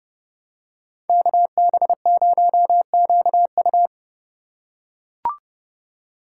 Morse code, heard as a keyed tone, about 29 wpm, 700 Hz. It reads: KB0QU